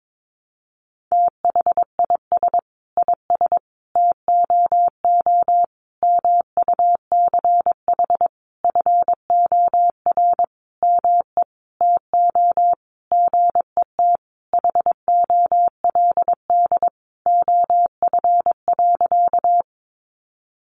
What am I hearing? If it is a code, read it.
Morse code, heard as a keyed tone, about 22 wpm, 700 Hz. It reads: THIS IS TOO MUCH FOR ME TO GET HOLD OF.
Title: A Sound Of Thunder 22wpm